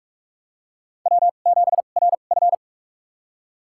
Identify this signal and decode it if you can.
Morse code, heard as a keyed tone, about 45 wpm, spaced as for 32 wpm, 700 Hz. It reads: W7RF
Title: USA Call Signs - 4 Characters - 2x Character Spacing - 45wpm